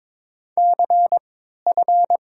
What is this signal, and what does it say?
Morse code, heard as a keyed tone, about 22 wpm, spaced as for 12 wpm, 700 Hz. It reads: CF
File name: Top 90 QSO Elements - 3x Character Spacing - 22wpm